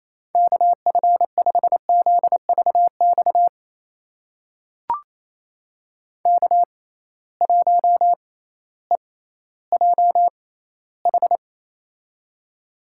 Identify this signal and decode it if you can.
Morse code, heard as a keyed tone, about 28 wpm, spaced as for 15 wpm, 700 Hz. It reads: KF5ZVX K1EJH
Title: USA Call Signs - 6x Character Spacing - 28wpm